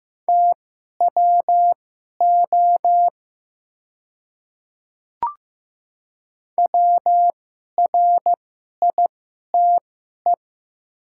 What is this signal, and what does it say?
Morse code, heard as a keyed tone, about 15 wpm, spaced as for 11 wpm, 700 Hz. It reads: TWO WRITE
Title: 2x Character Spacing - Top 200 Words - 15wpm